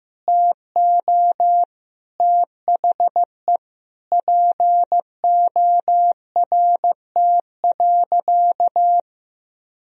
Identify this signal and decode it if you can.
Morse code, heard as a keyed tone, about 15 wpm, 700 Hz. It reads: TO THE PORT.
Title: Sentences from Top 1000 Words - Review Part 1 - 15wpm